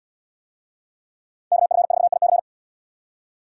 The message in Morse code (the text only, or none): KF4IC